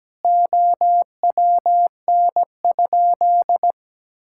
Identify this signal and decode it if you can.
Morse code, heard as a keyed tone, about 17 wpm, 700 Hz. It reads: OWN?
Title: Sentences from Top 1000 Words - Review Part 5 - 17wpm